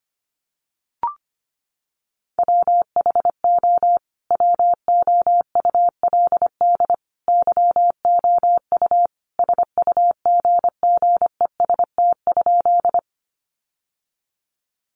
Morse code, heard as a keyed tone, about 25 wpm, 700 Hz. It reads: WHO WOULD YOU SUGGEST?
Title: Sentences from Top 1300 Words 25wpm